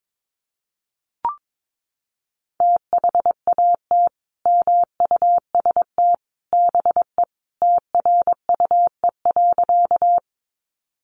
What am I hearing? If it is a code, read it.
Morse code, heard as a keyed tone, about 22 wpm, 700 Hz. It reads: THAT MUST BE TRUE.
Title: Sentences from Top 400 Words 22wpm